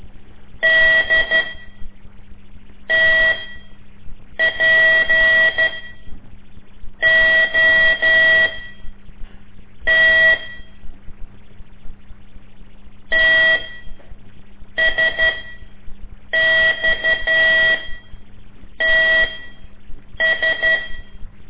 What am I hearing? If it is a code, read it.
Morse code, heard as a keyed tone, about 12 wpm, spaced as for 6 wpm, 1950 Hz. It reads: D TPOT TSXTS
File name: ik1weg2 / CORSI CW / lez5-1